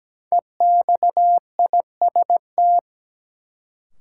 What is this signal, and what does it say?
Morse code, heard as a keyed tone, about 17 wpm, 700 Hz. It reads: EXIST
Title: Sentences from Top 1100 Words - New Word Drill Alt - 17wpm